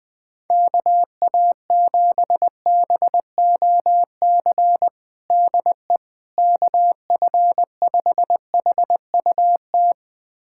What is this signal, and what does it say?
Morse code, heard as a keyed tone, about 21 wpm, 700 Hz. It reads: KA7BOC DE KF5HUT
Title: USA Callsign DE Callsign - 6 Characters - 20wpm